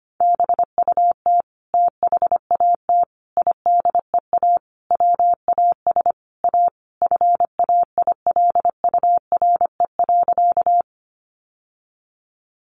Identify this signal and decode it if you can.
Morse code, heard as a keyed tone, about 25 wpm, 700 Hz. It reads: BUT THAT IDEA WAS A FAILURE.
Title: Sentences from Top 900 Words 25wpm